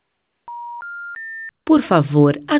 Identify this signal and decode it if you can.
Special information tone, with speech.